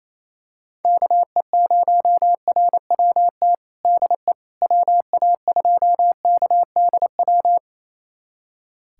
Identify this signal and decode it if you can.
Morse code, heard as a keyed tone, about 27 wpm, 700 Hz. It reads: KE0RWT DE WA2KDW
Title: USA Callsign DE Callsign - 6 Characters - 28wpm